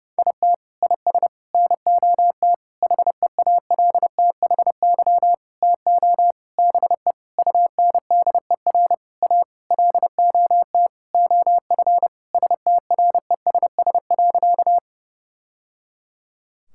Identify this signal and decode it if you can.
Morse code, heard as a keyed tone, about 30 wpm, 700 Hz. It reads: IT IS NOT HEALTHY TO BE UNDER A LOT OF STRESS.
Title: Sentences from Top 1000 Words - Review Part 6 - 30wpm